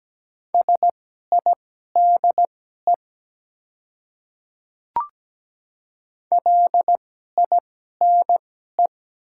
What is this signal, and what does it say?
Morse code, heard as a keyed tone, about 17 wpm, spaced as for 12 wpm, 700 Hz. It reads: SIDE LINE
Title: 2x Character Spacing - Top 200 Words - 17wpm